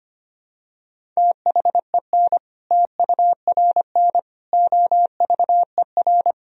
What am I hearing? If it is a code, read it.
Morse code, heard as a keyed tone, about 25 wpm, 700 Hz. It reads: THEN TURN OVER
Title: Sentences from Top 1000 Words - Review Part 1 - 25wpm